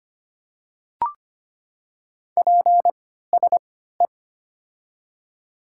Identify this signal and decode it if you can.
Morse code, heard as a keyed tone, about 25 wpm, spaced as for 14 wpm, 700 Hz. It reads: PSE
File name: Top 90 QSO Elements - 3x Character Spacing - 25wpm